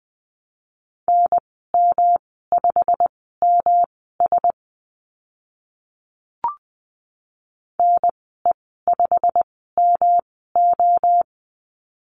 Morse code, heard as a keyed tone, about 20 wpm, spaced as for 14 wpm, 700 Hz. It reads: NM5MS NE5MO
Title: 140 Lesson 23g - Letter f - Call Signs 20wpm